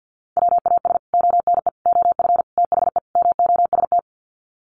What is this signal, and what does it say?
Morse code, heard as a keyed tone, about 50 wpm, 700 Hz. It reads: WAS ONE OF THE MOST
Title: Sets of 5 Words 50wpm